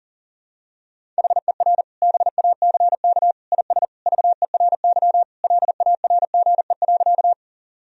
Morse code, heard as a keyed tone, about 40 wpm, 700 Hz. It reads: HER BACK IS VERY LARGE.
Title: Sentences from Top 1000 Words - Review Part 1 - 40wpm